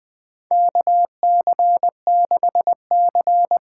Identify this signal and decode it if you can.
Morse code, heard as a keyed tone, about 20 wpm, 700 Hz. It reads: KC6C
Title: USA Call Signs - 6 Characters - 8x Character Spacing - 20wpm